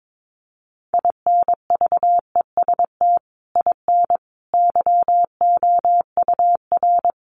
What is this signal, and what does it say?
Morse code, heard as a keyed tone, about 22 wpm, 700 Hz. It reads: INVEST IN YOUR
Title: Sentences from Top 1100 Words 22wpm